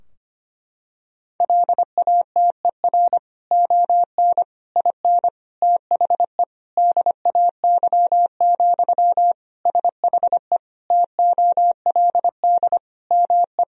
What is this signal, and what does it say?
Morse code, heard as a keyed tone, about 25 wpm, 700 Hz. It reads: LATER ON IN THE DAY, SHE TOLD ME
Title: Sentences from Top 1000 Words - Review Part 4 - 25wpm